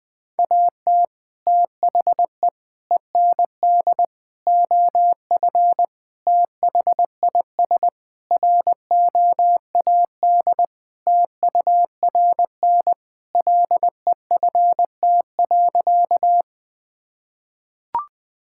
Morse code, heard as a keyed tone, about 20 wpm, 700 Hz. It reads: AT THE END OF THIS ROAD TURN LEFT.
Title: Sentences with 5-Word Phrases - 20wpm